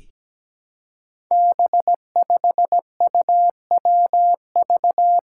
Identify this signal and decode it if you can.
Morse code, heard as a keyed tone, about 17 wpm, 700 Hz. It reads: B5UWV
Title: Five Challenging Letter-Number Combinations 17wpm